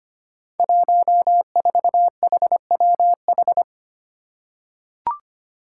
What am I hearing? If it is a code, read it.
Morse code, heard as a keyed tone, about 25 wpm, 700 Hz. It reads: 14HWH
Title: Five Challenging Letter-Number Combinations 25wpm